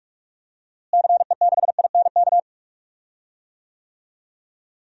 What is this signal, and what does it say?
Morse code, heard as a keyed tone, about 45 wpm, 700 Hz. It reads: CE6INK
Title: DXCC-100 Call Signs - 6 Characters - Repeated 2x - 45wpm